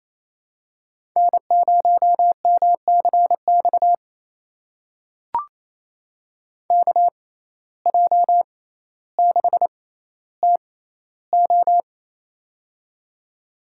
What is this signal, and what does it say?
Morse code, heard as a keyed tone, about 28 wpm, spaced as for 14 wpm, 700 Hz. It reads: N0MCX KJ6TO
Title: USA Call Signs - 5 Characters - 6x Character Spacing - 28wpm